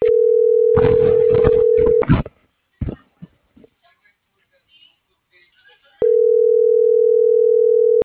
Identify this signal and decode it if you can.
Ringback tone (North America), 440 + 480 Hz.
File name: CENTRAL SAGOT / out miriam.mercado - .96938